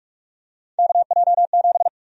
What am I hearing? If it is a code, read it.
Morse code, heard as a keyed tone, about 46 wpm, 700 Hz. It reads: KJ7
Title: USA Call Sign Prefixes - Repeated 2x - 45wpm